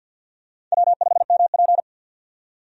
Morse code, heard as a keyed tone, about 46 wpm, 700 Hz. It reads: W5MP
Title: USA Call Signs - 4 Characters - 50wpm